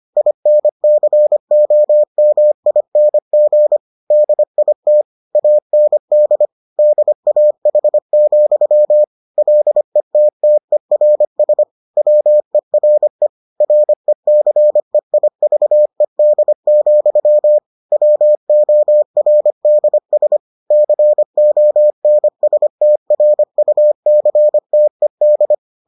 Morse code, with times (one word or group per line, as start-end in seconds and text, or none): 0.2-3.8 INCOMING
4.1-5.0 DIT
5.4-6.5 AND
6.8-9.0 DAH,
9.4-11.6 LETTERS
12.0-13.3 WERE
13.6-17.6 RECEIVED,
17.9-20.4 WORDS
20.7-25.5 CONSTRUCTED